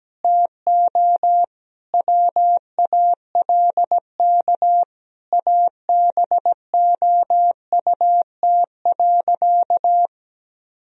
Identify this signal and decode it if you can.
Morse code, heard as a keyed tone, about 17 wpm, 700 Hz. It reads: TO WALK ABOUT.
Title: Sentences from Top 1000 Words - Review Part 1 - 17wpm